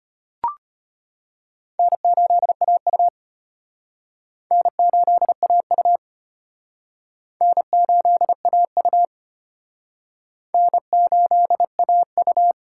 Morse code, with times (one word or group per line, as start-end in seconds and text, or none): 1.8-3.1 N8AU
4.5-6.0 N8AU
7.4-9.1 N8AU
10.5-12.5 N8AU